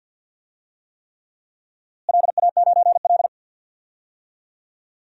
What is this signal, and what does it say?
Morse code, heard as a keyed tone, about 52 wpm, 700 Hz. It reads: LA9L
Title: DXCC-100 Call Signs - 4 Characters - Repeated 3x - 50wpm